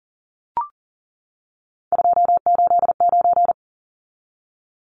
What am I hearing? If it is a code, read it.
Morse code, heard as a keyed tone, about 40 wpm, 700 Hz. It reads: 289
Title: Numbers - 3-Digit 40wpm